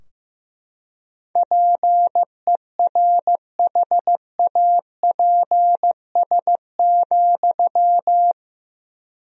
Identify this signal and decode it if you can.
Morse code, heard as a keyed tone, about 15 wpm, 700 Hz. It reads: PERHAPS,
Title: 3x Word-Spacing - Sentences from Top 500 Words - 15wpm